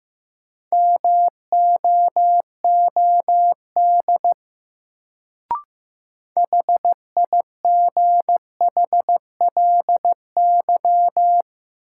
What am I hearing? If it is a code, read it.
Morse code, heard as a keyed tone, about 15 wpm, 700 Hz. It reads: MOOD HIGHLY
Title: Sentences from Top 1100 Words - New Word Drill Alt - 15wpm